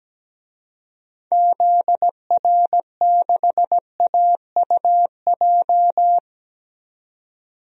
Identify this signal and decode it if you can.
Morse code, heard as a keyed tone, about 17 wpm, 700 Hz. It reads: ZR6AUJ
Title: DXCC-100 Call Signs - 6 Characters - 17wpm